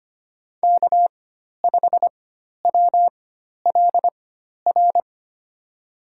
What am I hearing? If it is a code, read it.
Morse code, heard as a keyed tone, about 25 wpm, spaced as for 12 wpm, 700 Hz. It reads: K5WLR
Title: USA Call Signs - 4x Character Spacing - 25wpm